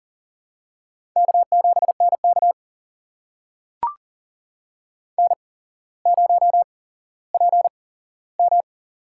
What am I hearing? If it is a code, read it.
Morse code, heard as a keyed tone, about 40 wpm, spaced as for 17 wpm, 700 Hz. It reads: K7NK N0PM
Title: USA Call Signs - 4 Characters - 8x Character Spacing - 40wpm